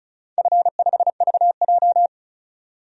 Morse code, heard as a keyed tone, about 36 wpm, 700 Hz. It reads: F5VJ
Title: DXCC-100 Call Signs - 4 Characters - 35wpm